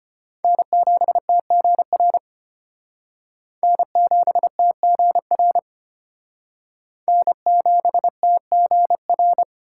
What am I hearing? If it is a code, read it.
Morse code, heard as a keyed tone, about 30 wpm, 700 Hz. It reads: N7TGR N7TGR N7TGR
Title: Speed Racing - USA 1x3 Call Signs 25wpm